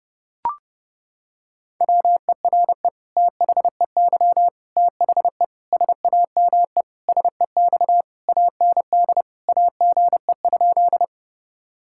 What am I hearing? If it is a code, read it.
Morse code, heard as a keyed tone, about 32 wpm, 700 Hz. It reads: WERE THEY THE SAME SEX AND AGE?
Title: Sentences from Top 1600 Words 30wpm